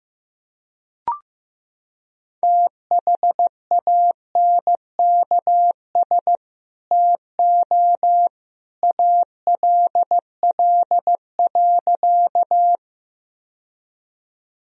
Morse code, heard as a keyed tone, about 15 wpm, 700 Hz. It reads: THANKS TO ALL.
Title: Sentences from Top 1000 Words - Review Part 3 - 15wpm